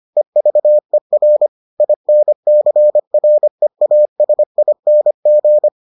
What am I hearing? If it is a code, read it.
Morse code, heard as a keyed tone, about 25 wpm, 600 Hz. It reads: EVER INCREASING